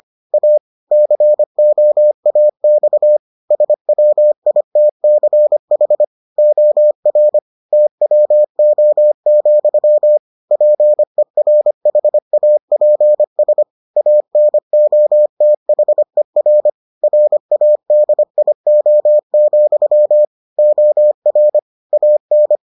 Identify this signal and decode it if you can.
Morse code, heard as a keyed tone, about 25 wpm, 600 Hz. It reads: A COAX SWITCH OR TWO, PERHAPS ANOTHER RADIO, OR AN